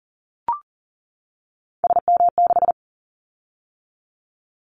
Morse code, heard as a keyed tone, about 38 wpm, 700 Hz. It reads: SM6
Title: Callsign Prefixes for Top 25 DXCC - 40wpm